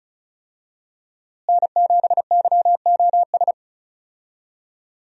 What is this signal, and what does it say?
Morse code, heard as a keyed tone, about 34 wpm, 700 Hz. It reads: N7YOS
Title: USA Call Signs - 5 Characters - Repeated 2x - 35wpm